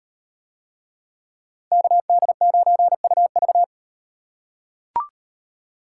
Morse code, heard as a keyed tone, about 38 wpm, 700 Hz. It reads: KD9UV